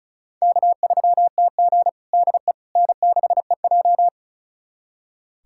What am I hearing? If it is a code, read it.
Morse code, heard as a keyed tone, about 36 wpm, 700 Hz. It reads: K3TG DE N6EJ
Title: USA Callsign DE Callsign - 4 Characters - 35wpm